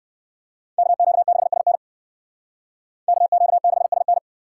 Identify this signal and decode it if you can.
Morse code, heard as a keyed tone, about 62 wpm, 700 Hz. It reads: KQ6SN KQ6SN